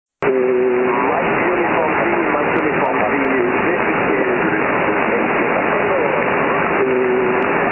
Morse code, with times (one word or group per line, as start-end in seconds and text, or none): none